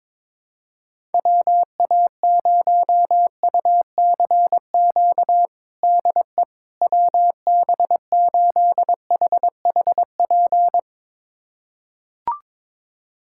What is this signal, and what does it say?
Morse code, heard as a keyed tone, about 22 wpm, 700 Hz. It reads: WA0UCQ DE WB8HHP
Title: USA Callsign DE Callsign - 6 Characters - 22wpm